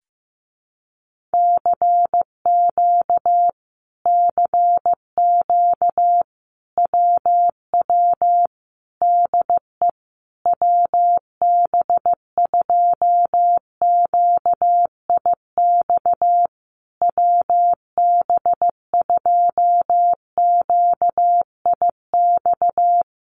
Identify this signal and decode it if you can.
Morse code, heard as a keyed tone, about 15 wpm, 700 Hz. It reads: CQ CQ WW DE WB2QIX WB2QIX
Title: Common QSO Phrases Encore 15wpm